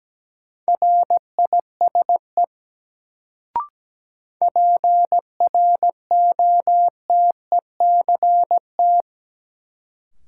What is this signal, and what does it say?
Morse code, heard as a keyed tone, about 17 wpm, 700 Hz. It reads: RISE PROTECT
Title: Sentences from Top 900 Words - New Word Drill - 17wpm